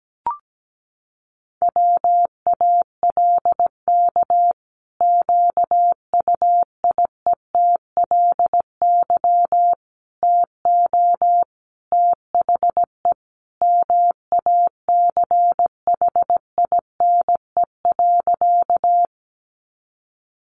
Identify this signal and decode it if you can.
Morse code, heard as a keyed tone, about 17 wpm, 700 Hz. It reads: WALK QUIETLY TO THE MACHINE.
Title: A Sound Of Thunder 17wpm